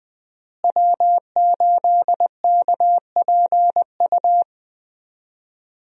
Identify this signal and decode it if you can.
Morse code, heard as a keyed tone, about 21 wpm, 700 Hz. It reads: W8KPU
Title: USA Call Signs - 5 Characters - Repeated 2x - 20wpm